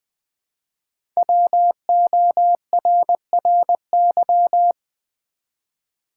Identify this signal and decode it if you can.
Morse code, heard as a keyed tone, about 20 wpm, 700 Hz. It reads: WORRY
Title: Sentences from Top 1100 Words - New Word Drill v2 - 20wpm